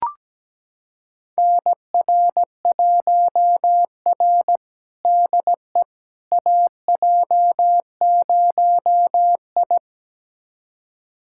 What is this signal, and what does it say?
Morse code, heard as a keyed tone, about 17 wpm, 700 Hz. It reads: NR1R DE AJ0I
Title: USA Callsign DE Callsign - 4 Characters - 17wpm